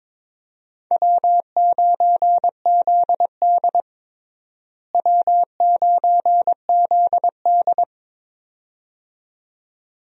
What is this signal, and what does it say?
Morse code, heard as a keyed tone, about 22 wpm, 700 Hz. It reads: W9ZD W9ZD
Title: USA Call Signs - 4 Characters - Repeated 2x - 22wpm